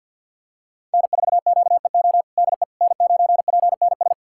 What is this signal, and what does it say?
Morse code, heard as a keyed tone, about 45 wpm, 700 Hz. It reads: N4QEO DE N9PNS
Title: USA Callsign DE Callsign - 5 Characters - 50wpm